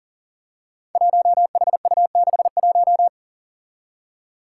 Morse code, heard as a keyed tone, about 40 wpm, 700 Hz. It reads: 1HU61
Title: Five Challenging Letter-Number Combinations 40wpm